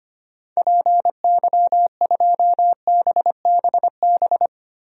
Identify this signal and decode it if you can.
Morse code, heard as a keyed tone, about 25 wpm, 700 Hz. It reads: PY2BBB
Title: DXCC-100 Call Signs - 6 Characters - Repeated 2x - 25wpm